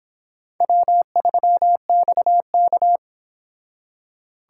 Morse code, heard as a keyed tone, about 25 wpm, 700 Hz. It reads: W3XK